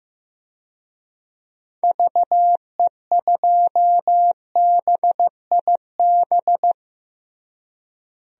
Morse code, heard as a keyed tone, about 15 wpm, 700 Hz. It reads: VE2BIB
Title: DXCC-100 Call Signs - 6 Characters - Repeated 2x - 15wpm